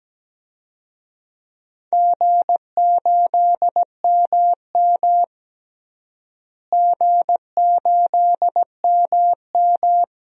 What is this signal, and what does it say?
Morse code, heard as a keyed tone, about 17 wpm, 700 Hz. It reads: G8MM G8MM
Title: DXCC-100 Call Signs - 4 Characters - Repeated 3x - 17wpm